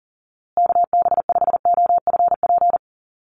Morse code, heard as a keyed tone, about 40 wpm, 700 Hz. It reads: KB5OFP
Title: USA Call Signs Repeated 2x 40wpm